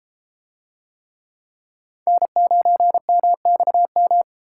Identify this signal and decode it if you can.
Morse code, heard as a keyed tone, about 33 wpm, 700 Hz. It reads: N9MXM